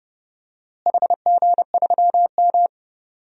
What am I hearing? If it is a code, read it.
Morse code, heard as a keyed tone, about 29 wpm, 700 Hz. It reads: HG3M
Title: DXCC-100 Call Signs - 4 Characters - Repeated 3x - 30wpm